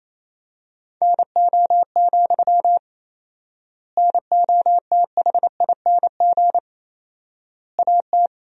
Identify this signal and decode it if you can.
Morse code, heard as a keyed tone, about 28 wpm, 700 Hz. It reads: NO, NOTHING AT
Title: 4x Word-Spacing - Sentences from Top 500 Words - 28wpm